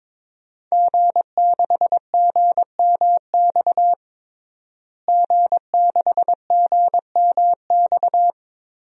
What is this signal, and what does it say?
Morse code, heard as a keyed tone, about 22 wpm, 700 Hz. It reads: G6GMX G6GMX
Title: DXCC-100 Call Signs - 5 Characters - Repeated 3x - 22wpm